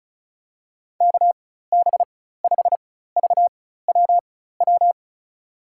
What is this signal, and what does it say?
Morse code, heard as a keyed tone, about 36 wpm, spaced as for 16 wpm, 700 Hz. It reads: KB5VWW
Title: USA Call Signs - 6 Characters - 4x Character Spacing - 35wpm